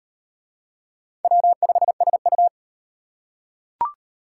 Morse code, heard as a keyed tone, about 38 wpm, 700 Hz. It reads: W5SU